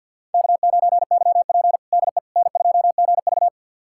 Morse code, heard as a keyed tone, about 52 wpm, 700 Hz. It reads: K9YP DE N2GV
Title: USA Callsign DE Callsign - 4 Characters - 50wpm